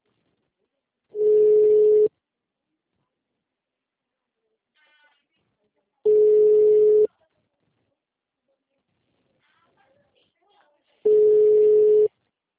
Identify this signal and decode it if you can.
Ringback tone (Europe, 425 Hz), with speech.
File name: out- -2041- - - .18653